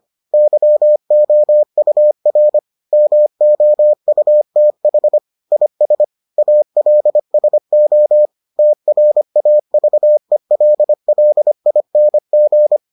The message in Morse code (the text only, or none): YOUR MOUTH IS ALSO TRAVELLING